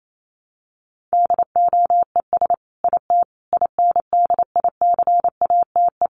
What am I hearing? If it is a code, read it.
Morse code, heard as a keyed tone, about 28 wpm, 700 Hz. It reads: DOES IT INDICATE
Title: Sentences from Top 1300 Words 28wpm